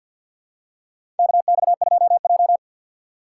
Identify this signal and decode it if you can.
Morse code, heard as a keyed tone, about 53 wpm, 700 Hz. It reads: KX2J
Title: USA Call Signs - 4 Characters - 3x Character Spacing - 50wpm